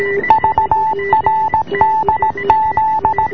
Morse code, heard as a keyed tone, about 17 wpm, 860 Hz, with noise. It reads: VRDZ